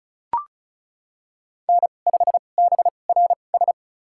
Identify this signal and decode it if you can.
Morse code, heard as a keyed tone, about 36 wpm, spaced as for 25 wpm, 700 Hz. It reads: N5BRS
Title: USA Call Signs - 5 Characters - 2x Character Spacing - 35wpm